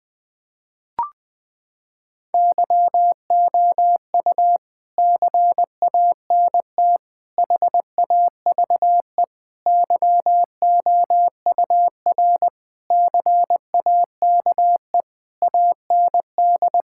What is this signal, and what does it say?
Morse code, heard as a keyed tone, about 21 wpm, 700 Hz. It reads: YOU CANT HAVE YOUR CAKE AND
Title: Sets of 9 Words - 20wpm